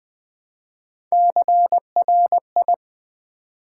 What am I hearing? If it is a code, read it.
Morse code, heard as a keyed tone, about 20 wpm, 700 Hz. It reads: CRI
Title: Build A Word - 1 Letter At A Time - 5-Letter Words - 20wpm